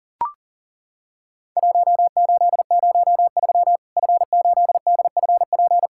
Morse code, heard as a keyed tone, about 40 wpm, 700 Hz. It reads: 1803 F8DFP
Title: Worked All Europe DX Contest - QTC Exchange - 40wpm